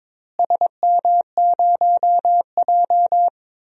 Morse code, heard as a keyed tone, about 22 wpm, 700 Hz. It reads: SM0J
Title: DXCC-100 Call Signs - 4 Characters - Repeated 3x - 22wpm